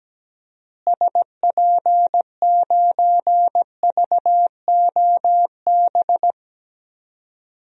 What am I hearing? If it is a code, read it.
Morse code, heard as a keyed tone, about 17 wpm, 700 Hz. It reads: SP9VOB
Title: DXCC-100 Call Signs - 6 Characters - Repeated 2x - 17wpm